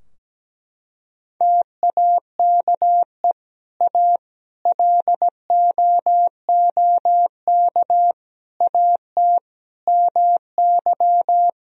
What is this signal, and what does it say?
Morse code, heard as a keyed tone, about 17 wpm, 700 Hz. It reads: TAKE A LOOK AT MY
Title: Sentences from Top 1000 Words - Review Part 3 - 17wpm